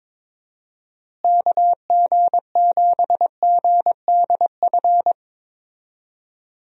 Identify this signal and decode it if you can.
Morse code, heard as a keyed tone, about 22 wpm, 700 Hz. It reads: KG7GDF
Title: USA Call Signs - 6 Characters - Repeated 3x - 22wpm